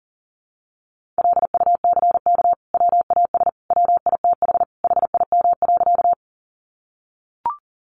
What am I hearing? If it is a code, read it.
Morse code, heard as a keyed tone, about 40 wpm, 700 Hz. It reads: LUCK WAS WITH HIM.
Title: Sentences from Top 1500 Words 40wpm